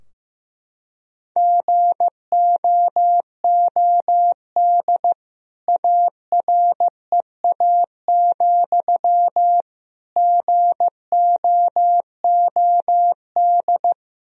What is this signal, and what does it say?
Morse code, heard as a keyed tone, about 15 wpm, 700 Hz. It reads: GOOD AREA, GOOD
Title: Sentences from Top 1000 Words - Review Part 2 - 15wpm